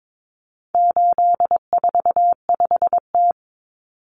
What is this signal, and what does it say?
Morse code, heard as a keyed tone, about 22 wpm, 700 Hz. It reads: 845T
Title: Common Cut Numbers - 4-Digit 22wpm